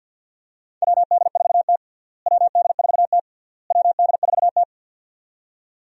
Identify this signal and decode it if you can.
Morse code, heard as a keyed tone, about 53 wpm, 700 Hz. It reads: WD4T WD4T WD4T
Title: USA Call Signs - 4 Characters - Repeated 3x - 50wpm